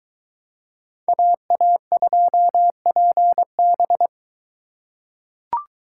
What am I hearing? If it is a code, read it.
Morse code, heard as a keyed tone, about 21 wpm, 700 Hz. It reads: AA2PB